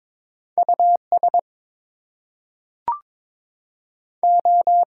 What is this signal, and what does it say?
Morse code, heard as a keyed tone, about 22 wpm, 700 Hz. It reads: US O
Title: Two Letter Words - 22wpm